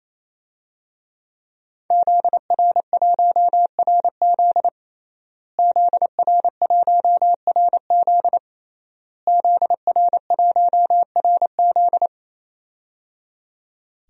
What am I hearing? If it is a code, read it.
Morse code, heard as a keyed tone, about 28 wpm, 700 Hz. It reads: ZR1RZ ZR1RZ ZR1RZ
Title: DXCC-100 Call Signs - 5 Characters - Repeated 3x - 28wpm